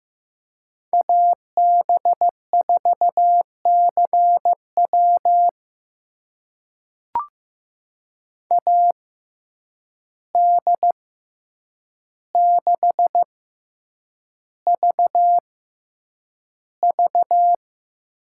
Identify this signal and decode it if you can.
Morse code, heard as a keyed tone, about 15 wpm, spaced as for 8 wpm, 700 Hz. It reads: AB4CW AD6VV
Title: USA Call Signs - 5 Characters - 6x Character Spacing - 15wpm